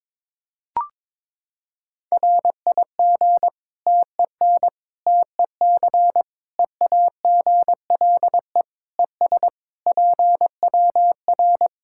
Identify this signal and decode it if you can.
Morse code, heard as a keyed tone, about 21 wpm, 700 Hz. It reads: RIG TEN TEC EAGLE ES PWR
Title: Practice QSO - Exchange 2 - Info - 22wpm